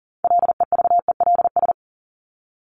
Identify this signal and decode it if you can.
Morse code, heard as a keyed tone, about 40 wpm, 700 Hz. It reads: LEVELS
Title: ES:S Suffix - Top 100 Words with Base Words 40wpm